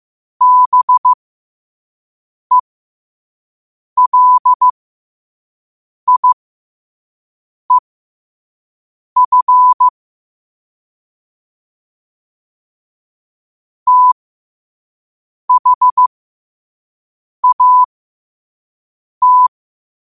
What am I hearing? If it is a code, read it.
Morse code, heard as a keyed tone, about 15 wpm, spaced as for 5 wpm, 1000 Hz. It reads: BELIEF THAT
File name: qotdfiles / 5wpm